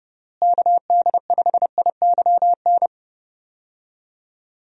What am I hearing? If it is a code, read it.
Morse code, heard as a keyed tone, about 30 wpm, 700 Hz. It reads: KD5IYN
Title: USA Call Signs - 6 Characters - Repeated 3x - 30wpm